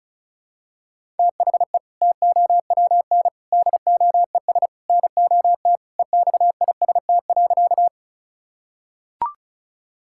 Morse code, heard as a keyed tone, about 36 wpm, 700 Hz. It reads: THE TOWN DOES NOT EXIST.
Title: Sentences from Top 1000 Words - Review Part 5 - 35wpm